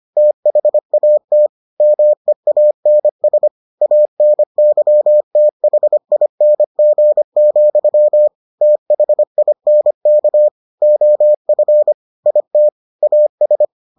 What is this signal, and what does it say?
Morse code, heard as a keyed tone, about 25 wpm, 600 Hz. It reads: THAT MEANS ANYTHING, THINK OF IT AS